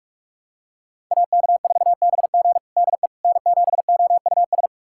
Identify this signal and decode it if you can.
Morse code, heard as a keyed tone, about 46 wpm, 700 Hz. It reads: AK4BG DE N7OUS
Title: USA Callsign DE Callsign - 5 Characters - 45wpm